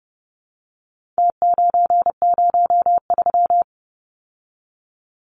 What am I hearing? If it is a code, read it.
Morse code, heard as a keyed tone, about 30 wpm, 700 Hz. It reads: T903
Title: Numbers - Contest Style - 4-Digit 30wpm